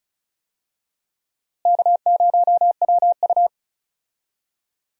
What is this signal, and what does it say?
Morse code, heard as a keyed tone, about 36 wpm, 700 Hz. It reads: K0WU
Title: USA Call Signs - 4 Characters - Repeated 2x - 35wpm